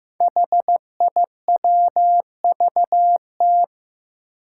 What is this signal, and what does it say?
Morse code, heard as a keyed tone, about 15 wpm, 700 Hz. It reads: HIWVT
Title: Five Challenging Letter-Number Combinations 15wpm